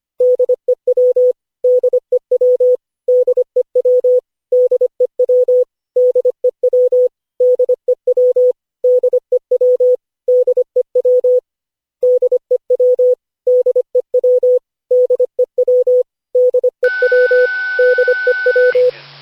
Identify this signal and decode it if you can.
Morse code, heard as a keyed tone, about 25 wpm, 500 Hz. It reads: DEW DEW DEW DEW DEW DEW DEW DEW DEW DEW DEW DEW DEW
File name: cw / Session5 cw training